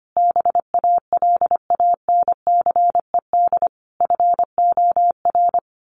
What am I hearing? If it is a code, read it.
Morse code, heard as a keyed tone, about 25 wpm, 700 Hz. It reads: BALANCED FOR
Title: A Sound Of Thunder 25wpm